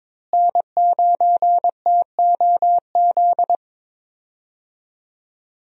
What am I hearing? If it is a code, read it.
Morse code, heard as a keyed tone, about 21 wpm, 700 Hz. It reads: N9TOZ
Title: USA Call Signs - 5 Characters - Repeated 3x - 22wpm